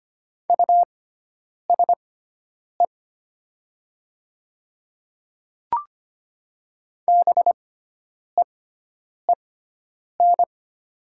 Morse code, heard as a keyed tone, about 25 wpm, spaced as for 9 wpm, 690 Hz. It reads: USE BEEN
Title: 6x Character Spacing - Top 200 Words - 25wpm